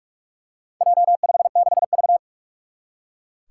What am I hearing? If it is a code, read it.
Morse code, heard as a keyed tone, about 41 wpm, 700 Hz. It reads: J56V
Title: Four Challenging Letter-Number Combinations 45wpm